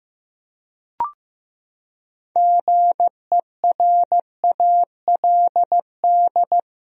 Morse code, heard as a keyed tone, about 15 wpm, 700 Hz. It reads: GERALD
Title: USA Common Names - 15wpm